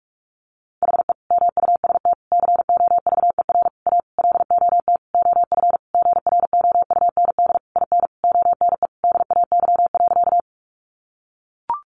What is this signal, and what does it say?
Morse code, heard as a keyed tone, about 44 wpm, 700 Hz. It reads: HE MUST COVER A LOT OF GROUND IN ONE DAY.
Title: Sentences from Top 500 Words 45wpm